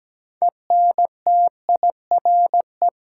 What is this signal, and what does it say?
Morse code, heard as a keyed tone, about 17 wpm, 700 Hz. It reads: ENTIRE
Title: Sentences from Top 800 Words - New Word Drill - 17wpm